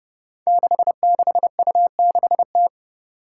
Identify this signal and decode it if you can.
Morse code, heard as a keyed tone, about 30 wpm, 700 Hz. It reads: 66U6T
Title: Five Challenging Letter-Number Combinations 30wpm